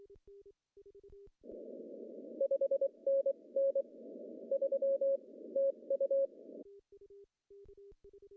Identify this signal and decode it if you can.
Morse code, heard as a keyed tone, about 23 wpm, spaced as for 20 wpm, 550 Hz. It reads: 5NN 3 TU
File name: CW / cqww-cw-k5tr